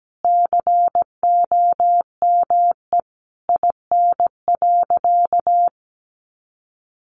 Morse code, heard as a keyed tone, about 17 wpm, 700 Hz. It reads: COME IN.
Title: Sentences from Top 1000 Words 17wpm